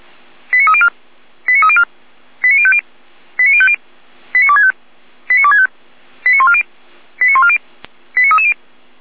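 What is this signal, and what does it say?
Five-tone selective call (ZVEI), heard as quick runs of five tones, tones of about 70 ms. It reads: 89304 89304 89050 89050 89265 89265 89140 89140 89330
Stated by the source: ZVEI five-tone selective call received via SDR.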